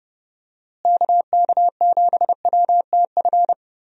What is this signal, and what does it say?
Morse code, heard as a keyed tone, about 30 wpm, 700 Hz. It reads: KK7WTF
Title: USA Call Signs - 6 Characters - Repeated 2x - 30wpm